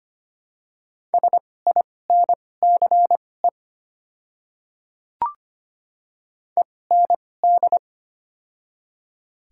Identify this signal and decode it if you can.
Morse code, heard as a keyed tone, about 25 wpm, spaced as for 18 wpm, 700 Hz. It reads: SINCE END
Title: 2x Character Spacing - Top 300 Words - 25wpm